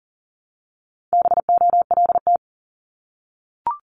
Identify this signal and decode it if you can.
Morse code, heard as a keyed tone, about 41 wpm, 700 Hz. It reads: BOLT
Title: Multi-letter Phonemes 40wpm